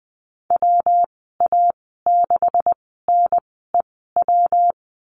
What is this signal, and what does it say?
Morse code, heard as a keyed tone, about 20 wpm, spaced as for 15 wpm, 700 Hz. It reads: WA6NEW
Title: 140 Lesson 23g - Letter f - Call Signs 20wpm